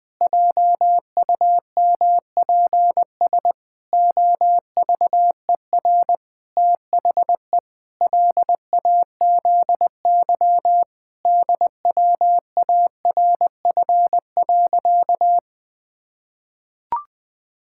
Morse code, heard as a keyed tone, about 20 wpm, 700 Hz. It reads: JUMPS OVER THE LAZY DWARF.
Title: Pangram Challenge v2 - 20wpm